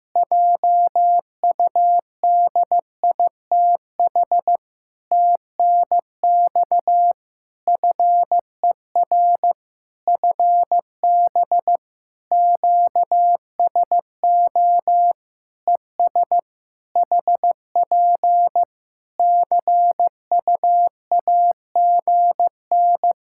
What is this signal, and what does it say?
Morse code, heard as a keyed tone, about 14 wpm, 700 Hz. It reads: JUDITH TNX FER FB QSO ES HP CUAGN
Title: Practice QSO - Exchange 4 - Ending - 15wpm